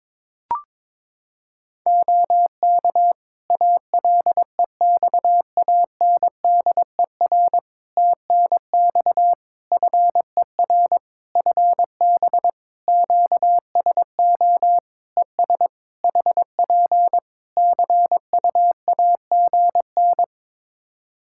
Morse code, heard as a keyed tone, about 22 wpm, 700 Hz. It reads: OK ALEXANDER TNX FER FB QSO ES HP CUAGN
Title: K1USN SST QSO Practice 22wpm